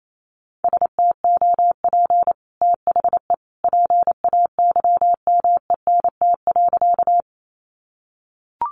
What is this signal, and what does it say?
Morse code, heard as a keyed tone, about 28 wpm, 700 Hz. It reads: STOP THE PAYMENT.
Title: Sentences from Top 1200 Words 28wpm